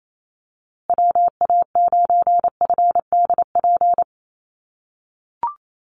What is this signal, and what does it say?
Morse code, heard as a keyed tone, about 28 wpm, 700 Hz. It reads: WA9FDP
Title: USA Call Signs Repeated 2x 28wpm